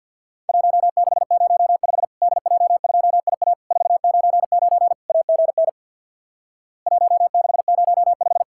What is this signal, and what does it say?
Morse code, heard as a keyed tone, about 50 wpm, 700 Hz. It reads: 1605 DJ2IA 499 AGN 1605
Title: Worked All Europe DX Contest - QTC Exchange with Repeats - 50wpm